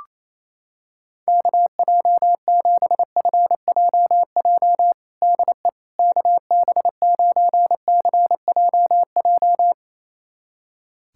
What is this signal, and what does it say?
Morse code, heard as a keyed tone, about 28 wpm, 700 Hz. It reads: KJ7FJJ DE KB9CJJ
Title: USA Callsign DE Callsign - 6 Characters - 28wpm